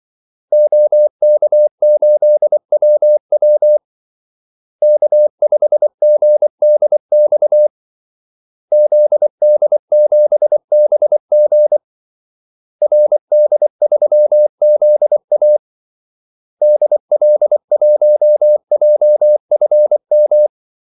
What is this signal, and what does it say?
Morse code, heard as a keyed tone, about 24 wpm, 600 Hz. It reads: OK8WW K5GDX ZD7BG RD3ZA DL1JFM
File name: Callsigns-24wpm